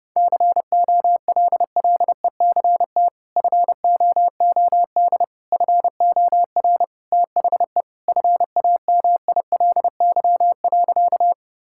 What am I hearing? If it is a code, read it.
Morse code, heard as a keyed tone, about 32 wpm, 700 Hz. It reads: COLLECT FOOD FOR THE FAMILY.
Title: Sentences from Top 1600 Words 30wpm